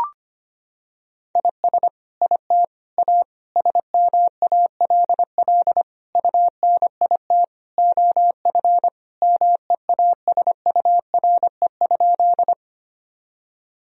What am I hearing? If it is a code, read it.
Morse code, heard as a keyed tone, about 25 wpm, 700 Hz. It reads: IS IT A SMALL UNIT OF MEASURE?
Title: Sentences from Top 1000 Words - Review Part 2 - 25wpm